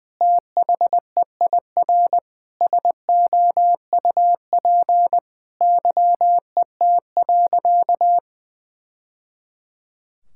Morse code, heard as a keyed tone, about 21 wpm, 700 Hz. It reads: THEIR SOUP YET.
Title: Sentences from Top 1000 Words - Review Part 6 - 20wpm